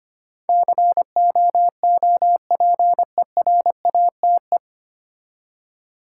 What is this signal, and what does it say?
Morse code, heard as a keyed tone, about 25 wpm, 700 Hz. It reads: COOPERATE
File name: practice / 25wpm / 9 Letter Words - First 3 Letters Given - 25wpm